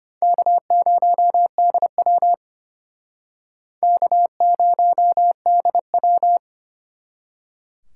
Morse code, heard as a keyed tone, about 29 wpm, 700 Hz. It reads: K0DW K0DW
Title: Speed Racing - USA 1x2 Call Signs 25wpm